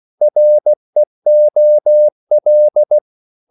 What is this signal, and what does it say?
Morse code, heard as a keyed tone, about 16 wpm, 600 Hz. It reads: REOL
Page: Korte danske ord 16wpm